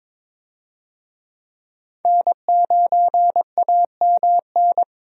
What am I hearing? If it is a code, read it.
Morse code, heard as a keyed tone, about 22 wpm, 700 Hz. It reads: N9AMN
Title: USA Call Signs - 5 Characters - 22wpm